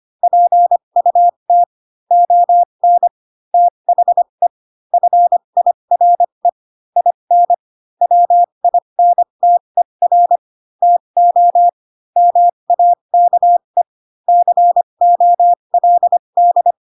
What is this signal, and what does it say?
Morse code, heard as a keyed tone, about 25 wpm, spaced as for 22 wpm, 700 Hz. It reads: PUT ON THE FIRE IN WINTER TO MAKE COLD